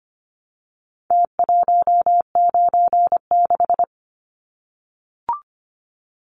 Morse code, heard as a keyed tone, about 25 wpm, 700 Hz. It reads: T196
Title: Numbers - Contest Style - 4-Digit 25wpm